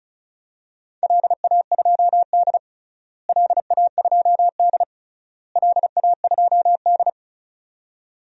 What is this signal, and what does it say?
Morse code, heard as a keyed tone, about 36 wpm, 700 Hz. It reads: LA2D LA2D LA2D
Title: DXCC-100 Call Signs - 4 Characters - Repeated 3x - 35wpm